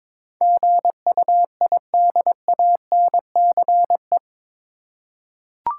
Morse code, heard as a keyed tone, about 21 wpm, 700 Hz. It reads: GUIDANCE
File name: practice / 22wpm / Eight Letter Words - 22wpm